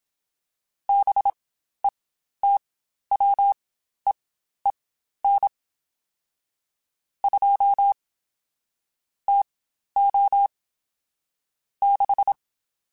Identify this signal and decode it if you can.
Morse code, heard as a keyed tone, about 27 wpm, spaced as for 12 wpm, 800 Hz. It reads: BETWEEN 2 TO 6
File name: Spaced Characters / Spaced Characters-mp3s / Weather 4 10 sp @25